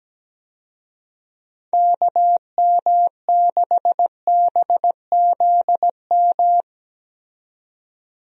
Morse code, heard as a keyed tone, about 17 wpm, 700 Hz. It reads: KM6BZM
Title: USA Call Signs - 6 Characters - 17wpm